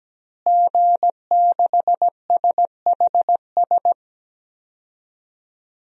Morse code, heard as a keyed tone, about 17 wpm, 700 Hz. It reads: G6SHS
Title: DXCC-100 Call Signs - 5 Characters - Repeated 3x - 17wpm